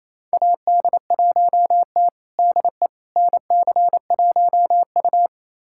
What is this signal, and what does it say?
Morse code, heard as a keyed tone, about 28 wpm, 700 Hz. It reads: AD1T DE NC1U
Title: USA Callsign DE Callsign - 4 Characters - 28wpm